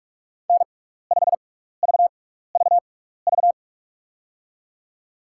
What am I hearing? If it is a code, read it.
Morse code, heard as a keyed tone, about 43 wpm, spaced as for 15 wpm, 700 Hz. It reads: N5VVV
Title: USA Call Signs - 5 Characters - 6x Character Spacing - 45wpm